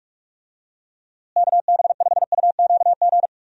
Morse code, heard as a keyed tone, about 46 wpm, 700 Hz. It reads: KB5UQG
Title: USA Call Signs - 6 Characters - Repeated 2x - 45wpm